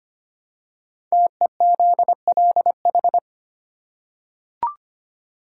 Morse code, heard as a keyed tone, about 25 wpm, 700 Hz. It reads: TEZLH
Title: Code Groups - 5 Letters - 25wpm